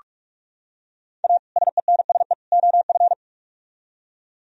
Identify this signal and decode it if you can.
Morse code, heard as a keyed tone, about 44 wpm, 700 Hz. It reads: A SENSE OF